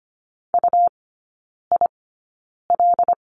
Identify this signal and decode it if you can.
Morse code, heard as a keyed tone, about 25 wpm, spaced as for 9 wpm, 700 Hz. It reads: UIL
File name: Lesson 17e - Letter u - Three Characters 25wpm